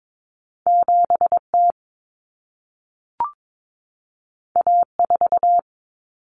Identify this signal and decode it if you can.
Morse code, heard as a keyed tone, about 22 wpm, 700 Hz. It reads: 7T A4
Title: Common Cut Numbers - 2-Digit 22wpm